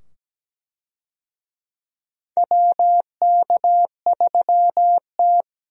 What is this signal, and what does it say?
Morse code, heard as a keyed tone, about 17 wpm, 700 Hz. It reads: WK3T
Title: USA Call Signs - 4 Characters - 17wpm